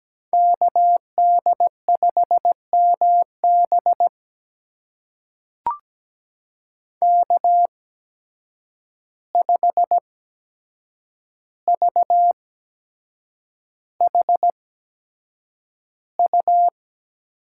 Morse code, heard as a keyed tone, about 17 wpm, spaced as for 7 wpm, 700 Hz. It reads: KD5MB K5VHU
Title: USA Call Signs - 5 Characters - 8x Character Spacing - 17wpm